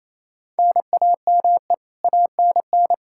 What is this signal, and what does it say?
Morse code, heard as a keyed tone, about 28 wpm, 700 Hz. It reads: NAME ANN
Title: K1USN SST QSO Practice 28wpm